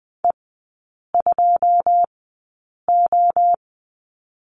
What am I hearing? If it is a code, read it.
Morse code, heard as a keyed tone, about 20 wpm, spaced as for 8 wpm, 700 Hz. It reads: E2O
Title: 087 Lesson 15e - Number 5 - Three Characters 20wpm